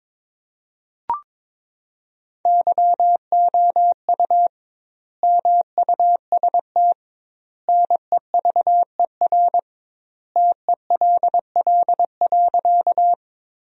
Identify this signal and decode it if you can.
Morse code, heard as a keyed tone, about 22 wpm, 700 Hz. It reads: YOU MUST NEVER TELL.
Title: 2x Word-Spacing - Sentences from Top 300 Words - 22wpm